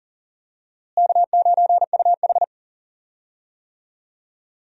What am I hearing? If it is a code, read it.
Morse code, heard as a keyed tone, about 40 wpm, 700 Hz. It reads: K9UH
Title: USA Call Signs - 4 Characters - Repeated 2x - 40wpm